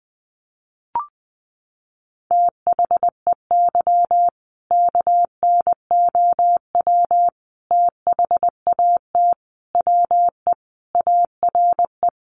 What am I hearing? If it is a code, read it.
Morse code, heard as a keyed tone, about 20 wpm, 700 Hz. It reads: THEY KNOW THAT WE ARE
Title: Sentences from Top 1400 Words 20wpm